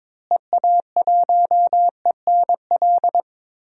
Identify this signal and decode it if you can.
Morse code, heard as a keyed tone, about 22 wpm, 700 Hz. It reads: EA1ENL
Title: DXCC-100 Call Signs - 6 Characters - Repeated 3x - 22wpm